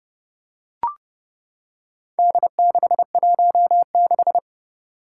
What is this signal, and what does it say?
Morse code, heard as a keyed tone, about 30 wpm, 700 Hz. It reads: D616
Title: Four Challenging Letter-Number Combinations 30wpm